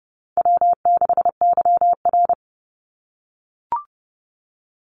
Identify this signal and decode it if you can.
Morse code, heard as a keyed tone, about 30 wpm, 700 Hz. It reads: W6YR
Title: Easy Call Signs - 30wpm